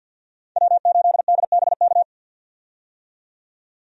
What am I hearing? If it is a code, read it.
Morse code, heard as a keyed tone, about 52 wpm, 700 Hz. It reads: W8DBK
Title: USA Call Signs - 5 Characters - 50wpm